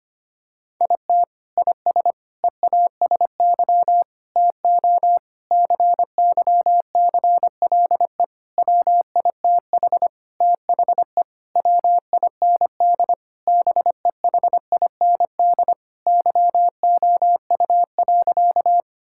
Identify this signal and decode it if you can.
Morse code, heard as a keyed tone, about 25 wpm, 700 Hz. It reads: IT IS EASY TO CYCLE WITH THE WIND BEHIND YOU.
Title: Sentences from Top 1000 Words - Review Part 6 - 25wpm